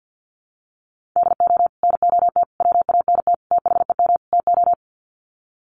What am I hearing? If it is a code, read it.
Morse code, heard as a keyed tone, about 50 wpm, 700 Hz. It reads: DO NOT WANT THEM TO
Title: Sets of 5 Words Encore 50wpm